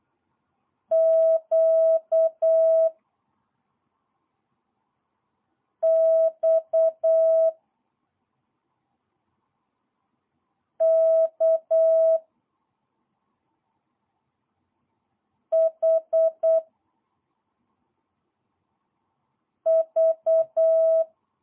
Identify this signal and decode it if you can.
Morse code, heard as a keyed tone, about 8 wpm, 660 Hz. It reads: Q X K H V
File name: jh8cbh / 2022cw